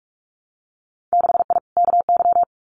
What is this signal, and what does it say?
Morse code, heard as a keyed tone, about 44 wpm, 700 Hz. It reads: 6I KY
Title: Winter Field Day Exchanges Encore 45wpm